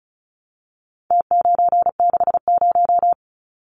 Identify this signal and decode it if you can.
Morse code, heard as a keyed tone, about 34 wpm, 700 Hz. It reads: T960
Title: Numbers - Contest Style - 4-Digit 35wpm